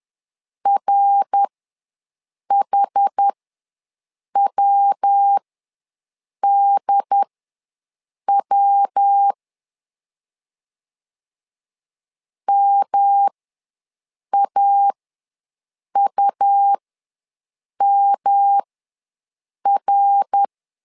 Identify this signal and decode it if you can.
Morse code, heard as a keyed tone, about 10 wpm, spaced as for 6 wpm, 800 Hz. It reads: RHWDW MAUMR
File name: LEZIONI / 3 MWRO / 3 MWR ws 10wpm 2151